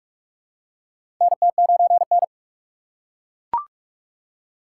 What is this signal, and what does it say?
Morse code, heard as a keyed tone, about 42 wpm, 700 Hz. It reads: NT9N
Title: USA Call Signs - 4 Characters - 2x Character Spacing - 45wpm